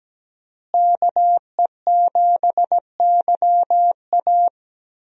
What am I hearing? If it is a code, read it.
Morse code, heard as a keyed tone, about 17 wpm, 700 Hz. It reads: KE7YA
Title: USA Call Signs - 5 Characters - Repeated 3x - 17wpm